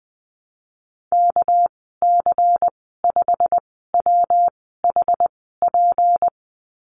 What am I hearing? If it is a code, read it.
Morse code, heard as a keyed tone, about 19 wpm, spaced as for 14 wpm, 700 Hz. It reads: KC5WHP
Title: 200 Lesson 32g - Letter k - Call Signs 20wpm